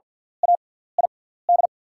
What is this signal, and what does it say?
Morse code, heard as a keyed tone, about 48 wpm, spaced as for 17 wpm, 690 Hz. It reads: AID
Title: 6x Character Spacing - Top 300 Words - 50wpm